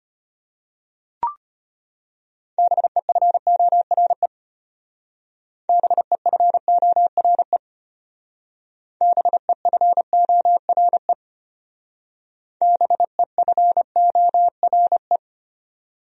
Morse code, heard as a keyed tone, about 32 wpm, 700 Hz. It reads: BEFORE BEFORE BEFORE BEFORE